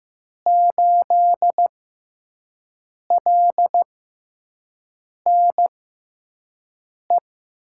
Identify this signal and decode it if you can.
Morse code, heard as a keyed tone, about 15 wpm, spaced as for 5 wpm, 700 Hz. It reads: 8LNE
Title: USA Call Signs - 5 Characters - 6x Character Spacing - 15wpm